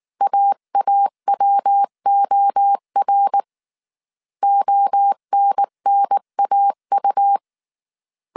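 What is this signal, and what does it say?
Morse code, heard as a keyed tone, about 19 wpm, 800 Hz. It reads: AAWOR ONNAU
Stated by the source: Morse code dictation at 18 wpm from the IW3SGT QRS Morse code training course, five-character groups of 300 characters.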